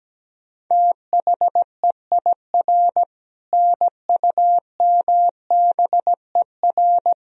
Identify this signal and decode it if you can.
Morse code, heard as a keyed tone, about 17 wpm, 700 Hz. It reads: THEIR NUMBER
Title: Sentences from Top 1000 Words - Review Part 1 - 17wpm